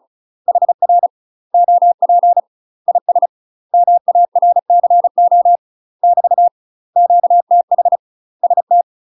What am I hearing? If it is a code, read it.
Morse code, heard as a keyed tone, about 34 wpm, 700 Hz. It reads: HR OP IS MARCO <BT> QTH ST
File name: UK MASTER QSOs TRAINING / 35WPM / UK MASTER QSOs 0004